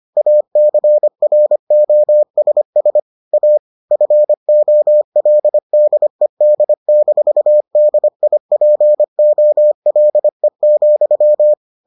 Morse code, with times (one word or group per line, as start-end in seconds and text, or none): 0.2-3.0 ACROSS
3.3-3.6 A
3.9-11.5 FOLDED-DIPOLE,